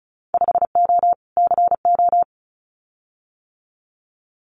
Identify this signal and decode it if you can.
Morse code, heard as a keyed tone, about 36 wpm, 700 Hz. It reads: 5O CO
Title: Winter Field Day Exchanges Encore 35wpm